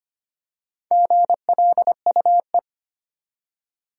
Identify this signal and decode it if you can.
Morse code, heard as a keyed tone, about 25 wpm, 700 Hz. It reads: GLUE
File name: practice / 25wpm / Easy Words with Hints - 25wpm